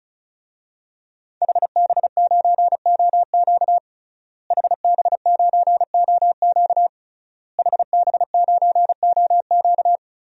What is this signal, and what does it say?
Morse code, heard as a keyed tone, about 36 wpm, 700 Hz. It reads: HB9OQ HB9OQ HB9OQ
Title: DXCC-100 Call Signs - 5 Characters - Repeated 3x - 35wpm